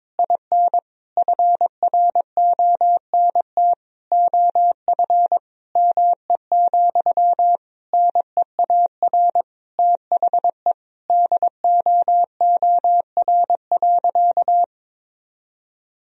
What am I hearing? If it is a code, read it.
Morse code, heard as a keyed tone, about 22 wpm, 700 Hz. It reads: IN FRONT OF ME, NEAR THE DOOR.
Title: Sentences from Top 1000 Words - Review Part 2 - 22wpm